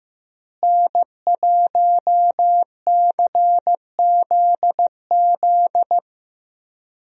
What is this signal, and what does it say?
Morse code, heard as a keyed tone, about 15 wpm, 700 Hz. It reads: N1CZZ
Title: USA Call Signs - 5 Characters - Repeated 3x - 15wpm